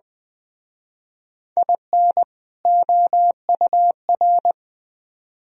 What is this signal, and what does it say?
Morse code, heard as a keyed tone, about 20 wpm, 700 Hz. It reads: IN OUR
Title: Speed-Racing - Sets of 2 Words 17wpm